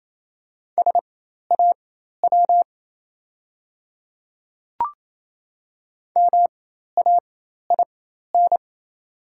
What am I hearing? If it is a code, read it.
Morse code, heard as a keyed tone, about 28 wpm, spaced as for 13 wpm, 700 Hz. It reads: SAW MAIN
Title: 4x Character Spacing - Top 300 Words - 28wpm